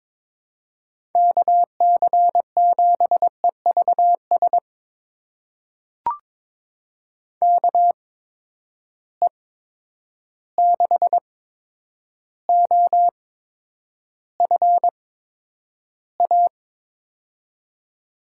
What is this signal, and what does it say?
Morse code, heard as a keyed tone, about 22 wpm, spaced as for 9 wpm, 700 Hz. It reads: KC7EVS KE6OFA
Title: USA Call Signs - 6 Characters - 8x Character Spacing - 22wpm